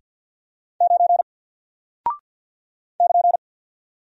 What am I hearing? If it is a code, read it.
Morse code, heard as a keyed tone, about 48 wpm, 700 Hz. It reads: 9 <KN>
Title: Single Character (Letter:Number:Punctuation:Prosign) Alt 50wpm